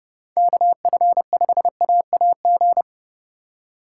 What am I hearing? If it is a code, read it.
Morse code, heard as a keyed tone, about 28 wpm, 700 Hz. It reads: KF5AAG